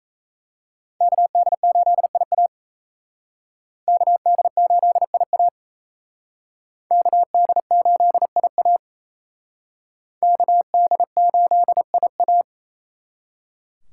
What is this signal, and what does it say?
Morse code, heard as a keyed tone, about 37 wpm, 700 Hz. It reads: KD8IA KD8IA KD8IA KD8IA